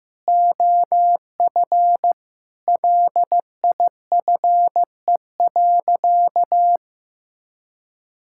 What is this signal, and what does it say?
Morse code, heard as a keyed tone, about 15 wpm, 700 Hz. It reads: OF LIFE.
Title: Sentences from Top 1000 Words - Review Part 6 - 15wpm